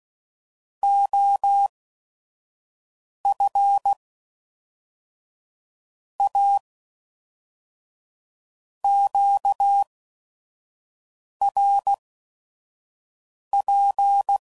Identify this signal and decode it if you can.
Morse code, heard as a keyed tone, about 16 wpm, spaced as for 5 wpm, 800 Hz. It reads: OF A QRP
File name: CW Code Groups @5WPM / QSOs / WD0FBT 1